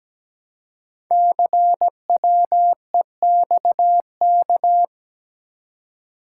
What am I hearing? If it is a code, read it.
Morse code, heard as a keyed tone, about 17 wpm, 700 Hz. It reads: CWEXK
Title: Code Groups - 5 Letters - 17wpm